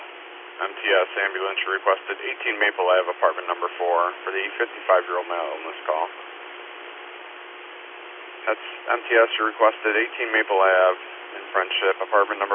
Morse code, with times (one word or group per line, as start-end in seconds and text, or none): none